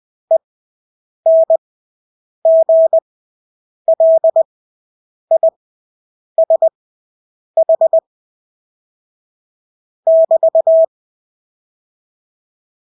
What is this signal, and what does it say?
Morse code, heard as a keyed tone, about 20 wpm, spaced as for 8 wpm, 650 Hz. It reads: ENGLISH <BT>
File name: Engelske ord 20/8 wpm